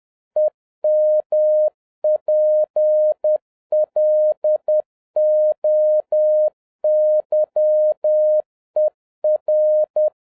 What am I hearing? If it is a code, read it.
Morse code, heard as a keyed tone, about 10 wpm, 610 Hz. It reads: EMPLOYER
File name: 2011-07-mp3 / n31 THE 3.11 2 10wpm